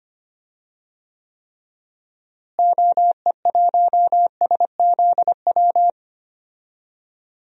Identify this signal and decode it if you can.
Morse code, heard as a keyed tone, about 25 wpm, 700 Hz. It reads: OE1SZW
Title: DXCC-100 Call Signs - 6 Characters - Repeated 3x - 25wpm